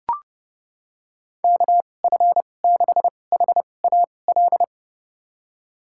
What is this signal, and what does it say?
Morse code, heard as a keyed tone, about 30 wpm, spaced as for 22 wpm, 700 Hz. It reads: KF6HAL
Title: USA Call Signs - 6 Characters - 2x Character Spacing - 30wpm